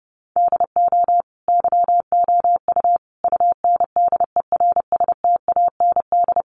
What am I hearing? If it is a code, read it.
Morse code, heard as a keyed tone, about 30 wpm, 700 Hz. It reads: DO YOU UNDERSTAND
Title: Sentences from Top 1200 Words 30wpm